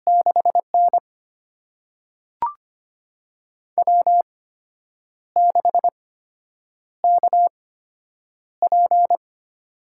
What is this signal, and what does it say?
Morse code, heard as a keyed tone, about 25 wpm, spaced as for 8 wpm, 700 Hz. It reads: 6N W6KP
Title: USA Call Signs - 4 Characters - 8x Character Spacing - 25wpm